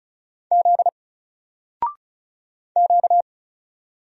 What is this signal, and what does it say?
Morse code, heard as a keyed tone, about 35 wpm, 700 Hz. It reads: Z Q
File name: Single-Letter-Number-35wpm